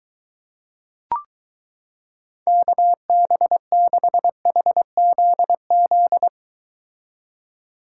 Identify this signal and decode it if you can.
Morse code, heard as a keyed tone, about 23 wpm, 700 Hz. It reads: KB6HZZ